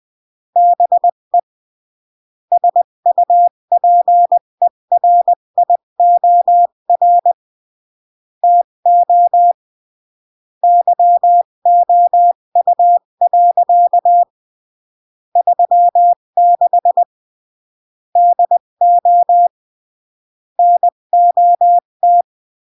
Morse code, heard as a keyed tone, about 20 wpm, 700 Hz. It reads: BE SUPERIOR TO YOU. 36 DO NOT
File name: CW / jodorowsky / 82 commandments 20-15